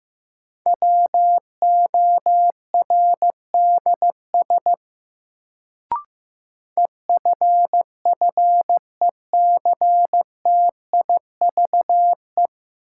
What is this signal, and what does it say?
Morse code, heard as a keyed tone, about 15 wpm, 700 Hz. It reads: WORDS EFFECTIVE
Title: Sentences from Top 1000 Words - Review Part 6 - New Word Drill - 15wpm